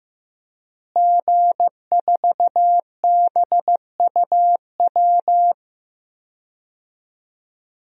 Morse code, heard as a keyed tone, about 15 wpm, 700 Hz. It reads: G4BUW
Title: DXCC-100 Call Signs - 5 Characters - Repeated 3x - 15wpm